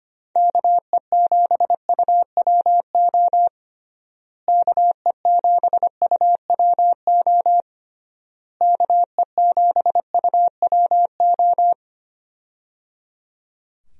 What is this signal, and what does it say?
Morse code, heard as a keyed tone, about 25 wpm, 700 Hz. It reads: KE7UWO KE7UWO KE7UWO
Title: USA Call Signs - 6 Characters - Repeated 3x - 25wpm